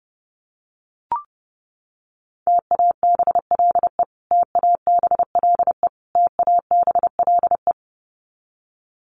Morse code, heard as a keyed tone, about 30 wpm, 700 Hz. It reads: TABLE TABLE TABLE
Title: Top 500 Words Repeated 30wpm